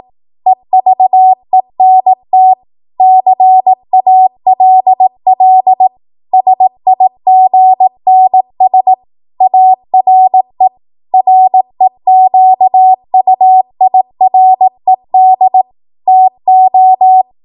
Morse code, heard as a keyed tone, about 18 wpm, 750 Hz. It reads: EVENT CALL SIGNS ARE REQUIRED TO
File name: Morse / 18WPM